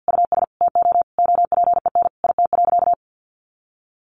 Morse code, heard as a keyed tone, about 50 wpm, 700 Hz. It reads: US TO OPEN IT.
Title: Sentences from Top 300 Words 50wpm